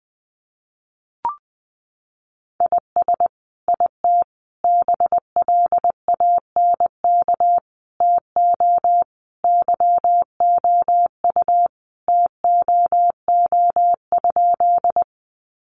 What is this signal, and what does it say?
Morse code, heard as a keyed tone, about 20 wpm, 700 Hz. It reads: IS IT BLANK TO YOU TOO?
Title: Sentences from Top 1500 Words 20wpm